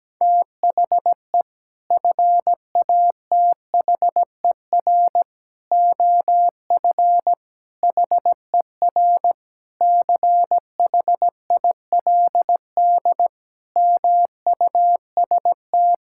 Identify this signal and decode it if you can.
Morse code, heard as a keyed tone, about 17 wpm, 700 Hz. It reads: THE FATHER OF HER CHILD MUST
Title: Sentences from Top 1000 Words - Review Part 5 - 17wpm